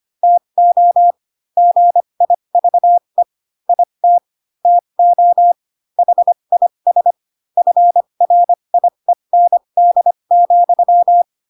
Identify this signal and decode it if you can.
Morse code, heard as a keyed tone, about 25 wpm, spaced as for 22 wpm, 700 Hz. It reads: TO GIVE IT TO HIS FRIEND,
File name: Pinocchio / Pinocchio ch2 18 25 ik1hge